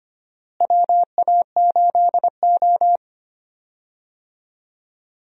Morse code, heard as a keyed tone, about 25 wpm, 700 Hz. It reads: WA8O
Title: USA Call Signs - 4 Characters - Repeated 3x - 25wpm